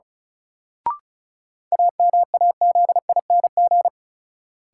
Morse code, heard as a keyed tone, about 36 wpm, 700 Hz. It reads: AMAZING
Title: Sentences from Top 1500 Words - New Word Drill Alt - 35wpm